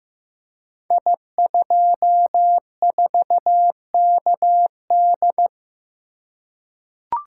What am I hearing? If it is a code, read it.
Morse code, heard as a keyed tone, about 15 wpm, 700 Hz. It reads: I24KD
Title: Code Groups - 5 Letter:Numbers - 15wpm